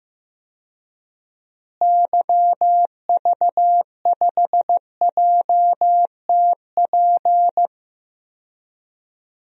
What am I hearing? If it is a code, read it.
Morse code, heard as a keyed tone, about 15 wpm, 700 Hz. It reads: YV5JTP
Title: DXCC-100 Call Signs - 6 Characters - Repeated 3x - 15wpm